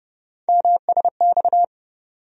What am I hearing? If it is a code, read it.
Morse code, heard as a keyed tone, about 30 wpm, 700 Hz. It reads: MSX
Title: USA Call Sign Postfixes - 3 Letters 30wpm